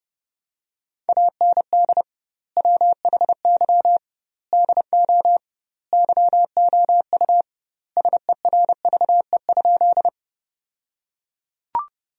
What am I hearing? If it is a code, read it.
Morse code, heard as a keyed tone, about 32 wpm, 700 Hz. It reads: AND WHY DO YOU SERVE?
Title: 2x Word-Spacing - Sentences from Top 400 Words - 30wpm